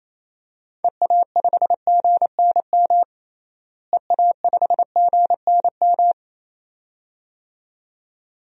Morse code, heard as a keyed tone, about 28 wpm, 700 Hz. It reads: EA5GNM EA5GNM
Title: DXCC-100 Call Signs - 6 Characters - Repeated 2x - 28wpm